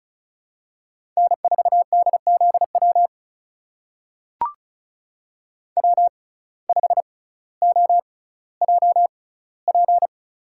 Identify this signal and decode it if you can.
Morse code, heard as a keyed tone, about 36 wpm, spaced as for 18 wpm, 700 Hz. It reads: N4DZW W5OJP
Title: USA Call Signs - 5 Characters - 6x Character Spacing - 35wpm